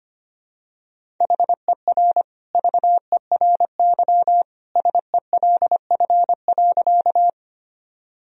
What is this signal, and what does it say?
Morse code, heard as a keyed tone, about 25 wpm, 700 Hz. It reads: HER VERY SELF.
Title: Sentences from Top 1000 Words - Review Part 1 - 25wpm